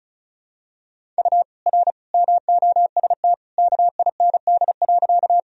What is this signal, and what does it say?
Morse code, heard as a keyed tone, about 35 wpm, 700 Hz. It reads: U R MOST KIND.
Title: Sentences from Top 200 Words and Abbreviations 35wpm